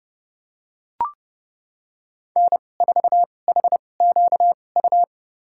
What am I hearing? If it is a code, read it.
Morse code, heard as a keyed tone, about 30 wpm, spaced as for 22 wpm, 700 Hz. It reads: N4HQU
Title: USA Call Signs - 5 Characters - 2x Character Spacing - 30wpm